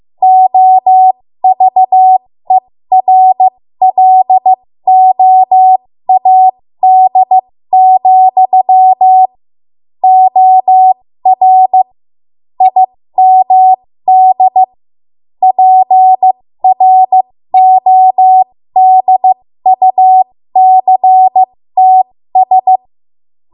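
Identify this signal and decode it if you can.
Morse code, heard as a keyed tone, about 15 wpm, spaced as for 13 wpm, 750 Hz. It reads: OVERLOAD, OR IMD PRODUCTS
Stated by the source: 13 wpm Morse code practice archive transmission.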